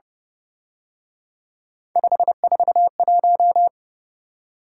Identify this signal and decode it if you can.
Morse code, heard as a keyed tone, about 29 wpm, 700 Hz. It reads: 541
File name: Speed-Racing - Three Letter-Number Combinations - 25wpm